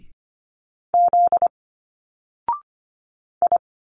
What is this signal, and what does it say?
Morse code, heard as a keyed tone, about 25 wpm, 700 Hz. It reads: Z I
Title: Single Letters 25wpm